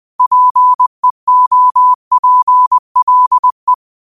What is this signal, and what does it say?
Morse code, heard as a keyed tone, about 20 wpm, 1000 Hz. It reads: PEOPLE